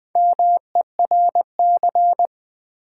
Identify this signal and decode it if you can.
Morse code, heard as a keyed tone, about 20 wpm, 700 Hz. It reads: MERC
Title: Build A Word - 1 Letter At A Time - 5-Letter Words - 20wpm